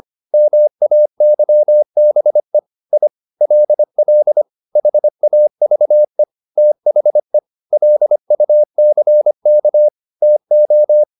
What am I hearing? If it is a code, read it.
Morse code, heard as a keyed tone, about 25 wpm, 600 Hz. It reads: MAYBE I LL HAVE THE LUCK TO